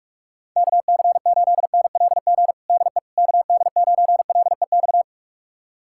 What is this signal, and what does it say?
Morse code, heard as a keyed tone, about 42 wpm, 700 Hz. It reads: KC8NLG DE KD9LEX
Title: USA Callsign DE Callsign - 6 Characters - 45wpm